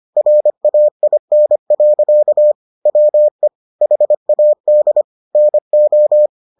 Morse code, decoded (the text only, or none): RAIN. WE HAD NO